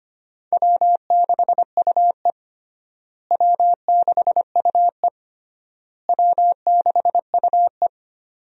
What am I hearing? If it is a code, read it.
Morse code, heard as a keyed tone, about 25 wpm, 700 Hz. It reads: W6UE W6UE W6UE
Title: USA Call Signs - 4 Characters - Repeated 3x - 25wpm